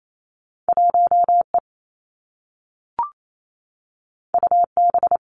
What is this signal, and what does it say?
Morse code, heard as a keyed tone, about 28 wpm, 700 Hz. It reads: 1 E U B
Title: Two Challenging Letter-Number Combinations 28wpm